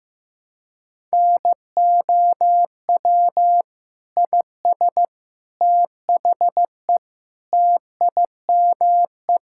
Morse code, heard as a keyed tone, about 15 wpm, 700 Hz. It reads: NOW IS THE TIME
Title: Pangram Challenge v2 - 15wpm